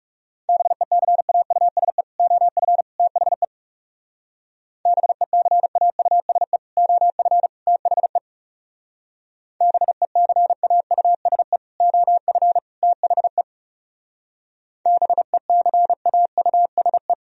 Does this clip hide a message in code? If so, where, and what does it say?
Yes: Morse code from 0.5 to 17.1 s, heard as a keyed tone, about 38 wpm, 700 Hz. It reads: BECAUSE OF THE BECAUSE OF THE BECAUSE OF THE BECAUSE